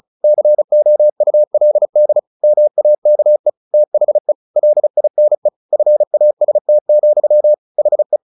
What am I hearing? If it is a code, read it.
Morse code, heard as a keyed tone, about 35 wpm, 600 Hz. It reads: COULD MAKE THE LINE FAST, HE
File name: the old man and the sea 2 35wpm